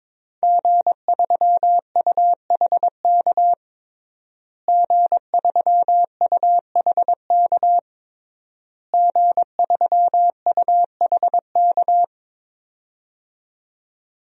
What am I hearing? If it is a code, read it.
Morse code, heard as a keyed tone, about 22 wpm, 700 Hz. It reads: G3UHK G3UHK G3UHK
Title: DXCC-100 Call Signs - 5 Characters - Repeated 3x - 22wpm